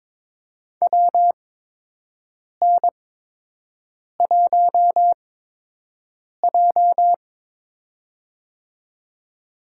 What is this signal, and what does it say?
Morse code, heard as a keyed tone, about 22 wpm, spaced as for 6 wpm, 700 Hz. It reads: WN1J
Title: USA Call Signs - 8x Character Spacing - 22wpm